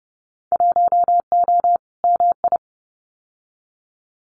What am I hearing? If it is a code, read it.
Morse code, heard as a keyed tone, about 30 wpm, 700 Hz. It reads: 1O MI
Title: Winter Field Day Exchanges 30wpm